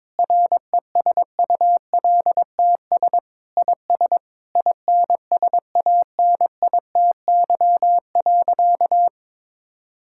Morse code, heard as a keyed tone, about 22 wpm, 700 Hz. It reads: RESULTS IS INSANITY.
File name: Sentences with 9-Word Phrases - 22wpm